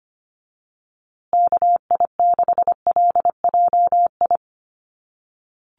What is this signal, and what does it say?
Morse code, heard as a keyed tone, about 25 wpm, 700 Hz. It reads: KI6LJI
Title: USA Call Signs Repeated 2x 25wpm